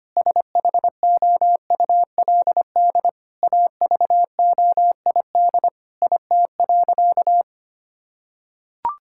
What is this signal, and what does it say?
Morse code, heard as a keyed tone, about 25 wpm, 700 Hz. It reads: SHOULD AVOID IT.
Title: Sentences from Top 1000 Words - Review Part 6 - 25wpm